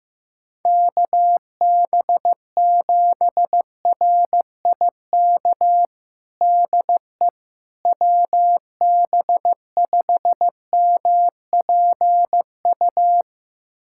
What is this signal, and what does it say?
Morse code, heard as a keyed tone, about 15 wpm, 700 Hz. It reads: KB7RIK DE WB5MPU
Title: USA Callsign DE Callsign - 6 Characters - 15wpm